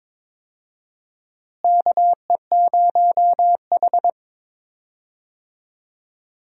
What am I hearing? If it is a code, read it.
Morse code, heard as a keyed tone, about 22 wpm, 700 Hz. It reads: KE0H
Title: USA Call Signs - 4 Characters - Repeated 2x - 22wpm